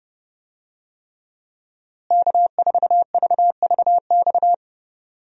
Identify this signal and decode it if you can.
Morse code, heard as a keyed tone, about 32 wpm, 700 Hz. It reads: K4VVX
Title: USA Call Signs - 5 Characters - Repeated 3x - 30wpm